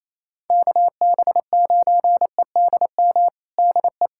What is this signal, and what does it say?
Morse code, heard as a keyed tone, about 28 wpm, 700 Hz. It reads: KB9EDM DE
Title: USA Callsign DE Callsign 28wpm